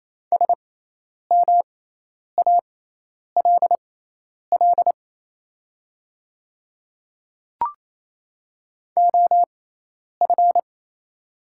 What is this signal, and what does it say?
Morse code, heard as a keyed tone, about 28 wpm, spaced as for 10 wpm, 700 Hz. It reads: SMALL OF
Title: 6x Character Spacing - Top 200 Words - 28wpm